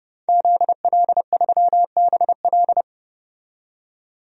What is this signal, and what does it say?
Morse code, heard as a keyed tone, about 31 wpm, 700 Hz. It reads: ZL3BL
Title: DXCC-100 Call Signs - 5 Characters - 30wpm